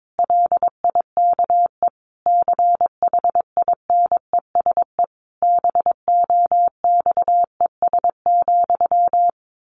Morse code, heard as a keyed tone, about 22 wpm, 700 Hz. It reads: LIKE CHINESE BOXES,
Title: A Sound Of Thunder 22wpm